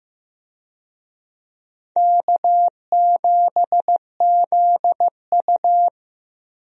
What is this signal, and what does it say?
Morse code, heard as a keyed tone, about 16 wpm, 700 Hz. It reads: K7ZU
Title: USA Call Signs - 4 Characters - Repeated 2x - 15wpm